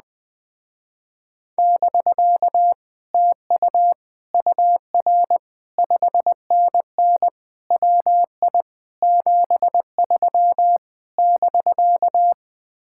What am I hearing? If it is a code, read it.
Morse code, heard as a keyed tone, about 20 wpm, 700 Hz. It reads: <BK> TU UR 5NN WI 73 <BK>
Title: Speed-Racing Common QSO Phrases - 17wpm